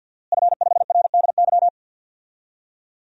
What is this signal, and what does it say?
Morse code, heard as a keyed tone, about 49 wpm, 700 Hz. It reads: F5RDY
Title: DXCC-100 Call Signs - 5 Characters - Repeated 3x - 50wpm